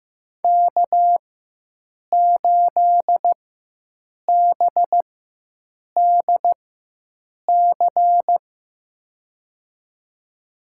Morse code, heard as a keyed tone, about 15 wpm, spaced as for 7 wpm, 700 Hz. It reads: K8BDC
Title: USA Call Signs - 5 Characters - 4x Character Spacing - 15wpm